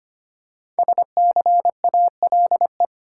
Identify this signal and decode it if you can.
Morse code, heard as a keyed tone, about 25 wpm, 700 Hz. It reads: SCALE
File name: 5 Letter Words - First 3 Letters Given - 25wpm